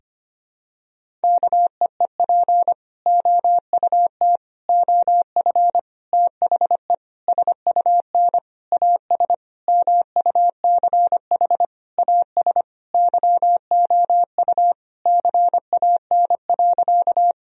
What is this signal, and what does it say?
Morse code, heard as a keyed tone, about 25 wpm, 700 Hz. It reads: KEEP OUT OF THE SUN AS MUCH AS YOU CAN.
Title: Sentences from Top 1000 Words - Review Part 1 - 25wpm